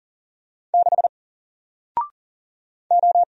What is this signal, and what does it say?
Morse code, heard as a keyed tone, about 40 wpm, 700 Hz. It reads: 6 O
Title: Single Letter-Number Alt 40wpm